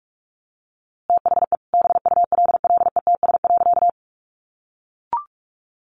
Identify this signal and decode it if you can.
Morse code, heard as a keyed tone, about 46 wpm, 700 Hz. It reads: THE BULLETS.
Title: A Sound Of Thunder 45wpm